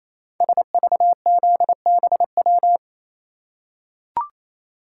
Morse code, heard as a keyed tone, about 29 wpm, 700 Hz. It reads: SVZBW
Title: Code Groups - 5 Letters - 28wpm